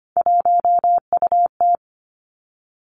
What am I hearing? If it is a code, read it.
Morse code, heard as a keyed tone, about 25 wpm, 700 Hz. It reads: 1UT
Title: Three Challenging Letter-Number Combinations 25wpm